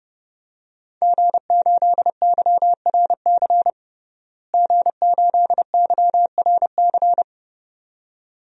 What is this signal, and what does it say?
Morse code, heard as a keyed tone, about 30 wpm, 700 Hz. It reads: G8YRC G8YRC
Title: DXCC-100 Call Signs - 5 Characters - Repeated 2x - 30wpm